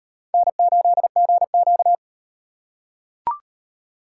Morse code, heard as a keyed tone, about 35 wpm, 700 Hz. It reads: N8GQ